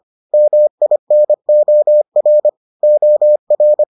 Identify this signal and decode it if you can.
Morse code, heard as a keyed tone, about 25 wpm, 600 Hz. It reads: MINOR OR